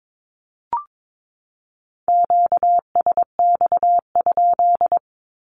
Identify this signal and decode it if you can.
Morse code, heard as a keyed tone, about 22 wpm, 700 Hz. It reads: QSX?
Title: Q-Codes 22wpm